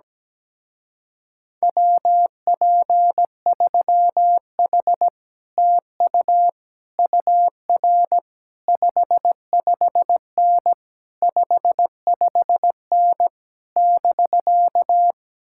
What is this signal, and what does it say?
Morse code, heard as a keyed tone, about 17 wpm, 700 Hz. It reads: WP3H TU UR 55N 55N <BK>
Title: Speed-Racing Common QSO Phrases - 17wpm